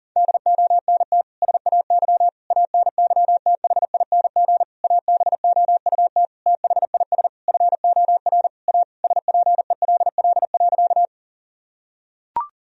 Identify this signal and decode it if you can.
Morse code, heard as a keyed tone, about 40 wpm, 700 Hz. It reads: DONT SAY ANYTHING ABOUT THIS FOR A SPELL.
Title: Sentences from Top 1000 Words - Review Part 1 - 40wpm